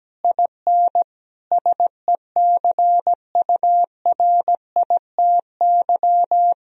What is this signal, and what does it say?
Morse code, heard as a keyed tone, about 17 wpm, 700 Hz. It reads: IN SECURITY
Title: Sentences from Top 1000 Words - Review Part 4 - 17wpm